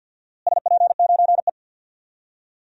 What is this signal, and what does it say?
Morse code, heard as a keyed tone, about 47 wpm, 700 Hz. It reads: SP9E
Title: DXCC-100 Call Signs - 4 Characters - Repeated 3x - 50wpm